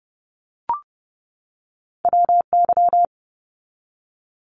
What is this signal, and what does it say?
Morse code, heard as a keyed tone, about 30 wpm, 700 Hz. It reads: WY
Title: US States 30wpm